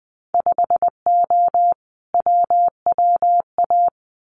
Morse code, heard as a keyed tone, about 20 wpm, 700 Hz. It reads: 5O WWA
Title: Winter Field Day Exchanges Encore 20wpm